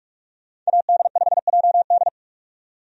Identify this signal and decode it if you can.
Morse code, heard as a keyed tone, about 46 wpm, 700 Hz. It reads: AD5JD
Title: USA Call Signs - 5 Characters - 45wpm